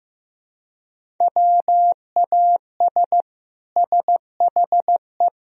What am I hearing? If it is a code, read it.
Morse code, heard as a keyed tone, about 16 wpm, 700 Hz. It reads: WAS SHE
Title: Sentences from Top 1000 Words - Review Part 3 - 15wpm